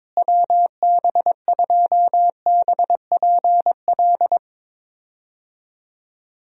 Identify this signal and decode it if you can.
Morse code, heard as a keyed tone, about 22 wpm, 700 Hz. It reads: WB2BPL
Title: USA Call Signs - 6 Characters - Repeated 3x - 22wpm